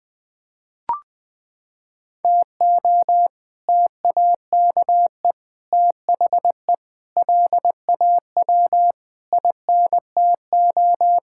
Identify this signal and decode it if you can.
Morse code, heard as a keyed tone, about 20 wpm, 700 Hz. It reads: TO TAKE THE LAW INTO
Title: Sets of 8 Words - 20wpm